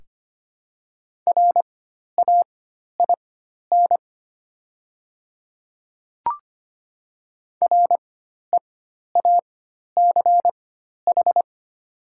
Morse code, heard as a keyed tone, about 25 wpm, spaced as for 12 wpm, 700 Hz. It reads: RAIN REACH
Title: 4x Character Spacing - Top 500 Words - 25wpm